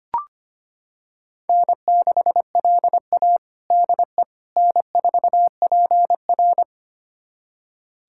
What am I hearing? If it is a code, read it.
Morse code, heard as a keyed tone, about 25 wpm, 700 Hz. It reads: N6LA DE N4PR
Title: USA Callsign DE Callsign - 4 Characters - 25wpm